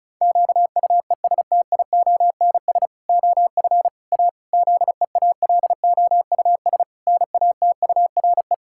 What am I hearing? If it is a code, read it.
Morse code, heard as a keyed tone, about 36 wpm, 700 Hz. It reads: QUESTIONS OF A ZEALOUS NATURE
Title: Pangram Challenge v2 - 35wpm